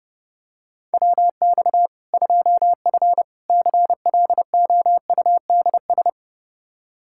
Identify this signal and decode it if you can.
Morse code, heard as a keyed tone, about 30 wpm, 700 Hz. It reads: WX 2F CLOUDS
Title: Practice QSO - Exchange 2 - Info - 30wpm